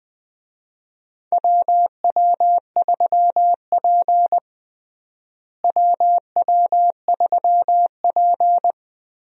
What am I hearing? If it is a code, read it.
Morse code, heard as a keyed tone, about 20 wpm, 700 Hz. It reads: WW3P WW3P
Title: USA Call Signs - 4 Characters - Repeated 3x - 20wpm